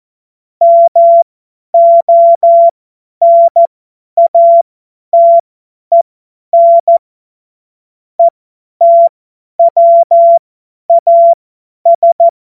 Morse code, heard as a keyed tone, about 14 wpm, spaced as for 10 wpm, 680 Hz. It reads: MONATEN ETWAS
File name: Witz-05-August-2025-50-F70